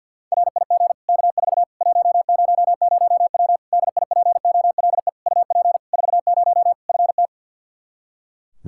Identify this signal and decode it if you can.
Morse code, heard as a keyed tone, about 48 wpm, 700 Hz. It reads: RIG K4 100W DIPOLE UP 40 FT
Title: Practice QSO - Exchange 2 - Info - 50wpm